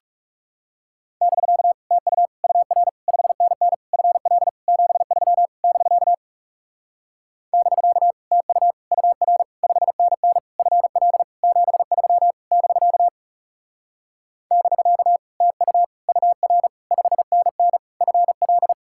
Morse code, heard as a keyed tone, about 38 wpm, 700 Hz. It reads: <BK> TU UR 5NN FL 73 <BK> <BK> TU UR 5NN FL 73 <BK> <BK> TU UR 5NN FL